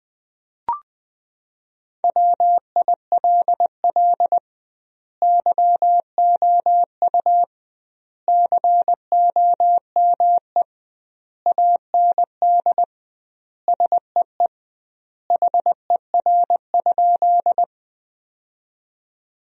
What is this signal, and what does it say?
Morse code, heard as a keyed tone, about 21 wpm, 700 Hz. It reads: WILL YOU COME AND SEE HER?
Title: 2x Word-Spacing - Sentences from Top 100 Words - 20wpm